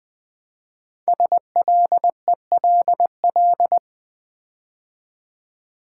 Morse code, heard as a keyed tone, about 20 wpm, 700 Hz. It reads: SLELL
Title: Code Groups - 5 Letters - 20wpm